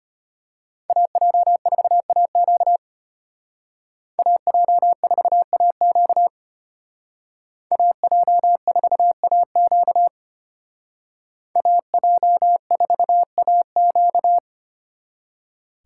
Morse code, heard as a keyed tone, about 33 wpm, 700 Hz. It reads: AJ4AQ AJ4AQ AJ4AQ AJ4AQ